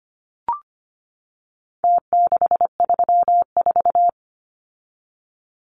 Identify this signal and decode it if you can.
Morse code, heard as a keyed tone, about 25 wpm, 700 Hz. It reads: T634
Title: Numbers - Contest Style - 4-Digit 25wpm